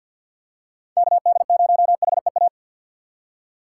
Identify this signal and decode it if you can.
Morse code, heard as a keyed tone, about 54 wpm, 700 Hz. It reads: KD0HEA
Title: USA Call Signs - 6 Characters - 8x Character Spacing - 50wpm